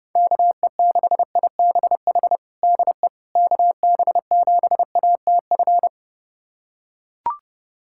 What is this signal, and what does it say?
Morse code, heard as a keyed tone, about 32 wpm, 700 Hz. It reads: KE6IBH DE KB7ATF
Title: USA Callsign DE Callsign - 6 Characters - 30wpm